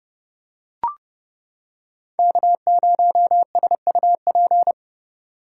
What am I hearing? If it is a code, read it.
Morse code, heard as a keyed tone, about 30 wpm, 700 Hz. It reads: K0SUP